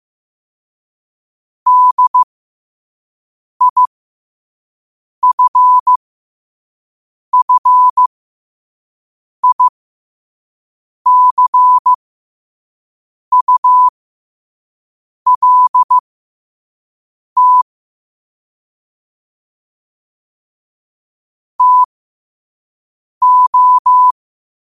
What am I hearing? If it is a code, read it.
Morse code, heard as a keyed tone, about 15 wpm, spaced as for 5 wpm, 1000 Hz. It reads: DIFFICULT TO
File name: qotdfiles / 5wpm